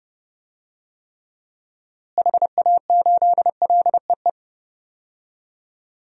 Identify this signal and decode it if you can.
Morse code, heard as a keyed tone, about 30 wpm, 700 Hz. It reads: HA8LEE
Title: DXCC-100 Call Signs - 6 Characters - Repeated 2x - 30wpm